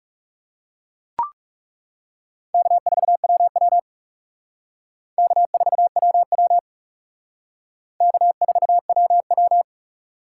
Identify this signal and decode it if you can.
Morse code, heard as a keyed tone, about 37 wpm, 700 Hz. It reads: K4WW K4WW K4WW
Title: Speed Racing - USA 1x2 Call Signs 30wpm